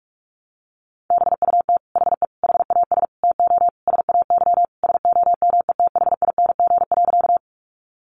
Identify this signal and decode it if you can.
Morse code, heard as a keyed tone, about 44 wpm, 700 Hz. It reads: BUT HE HAS TO SAY SOMETHING.
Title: Sentences from Top 600 Words 45wpm